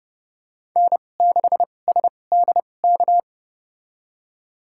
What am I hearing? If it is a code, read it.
Morse code, heard as a keyed tone, about 30 wpm, spaced as for 22 wpm, 700 Hz. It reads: N6SDK
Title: USA Call Signs - 2x Character Spacing - 30wpm